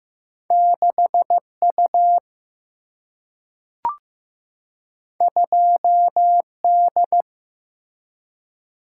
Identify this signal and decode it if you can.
Morse code, heard as a keyed tone, about 15 wpm, 700 Hz. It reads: 6U 2D
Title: Single-Letter Single-Number 15wpm